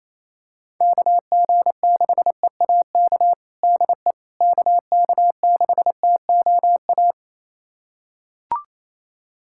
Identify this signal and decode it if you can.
Morse code, heard as a keyed tone, about 28 wpm, 700 Hz. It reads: KG6EAK DE KK6TOA
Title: USA Callsign DE Callsign - 6 Characters - 28wpm